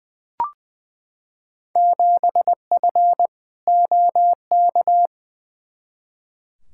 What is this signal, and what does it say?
Morse code, heard as a keyed tone, about 20 wpm, 700 Hz. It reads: 7F OK
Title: Field Day Exchanges Encore - 20wpm